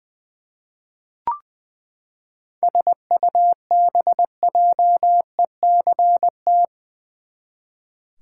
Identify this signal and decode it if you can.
Morse code, heard as a keyed tone, about 20 wpm, 700 Hz. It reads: SUBJECT
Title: Seven Letter Words - 20wpm